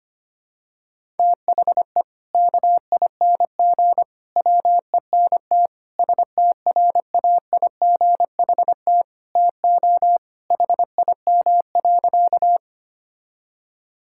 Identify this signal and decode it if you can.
Morse code, heard as a keyed tone, about 25 wpm, 700 Hz. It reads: THE KING WENT STRAIGHT TO HIM.
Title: Sentences from Top 1000 Words - Review Part 6 - 25wpm